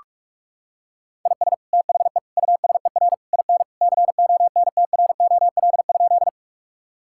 Morse code, heard as a keyed tone, about 46 wpm, 700 Hz. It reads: IS THE USER IN CONTROL?
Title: Sentences from Top 1000 Words - Review Part 5 - 45wpm